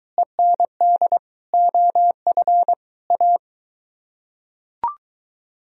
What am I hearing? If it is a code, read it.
Morse code, heard as a keyed tone, about 22 wpm, 700 Hz. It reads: END OF A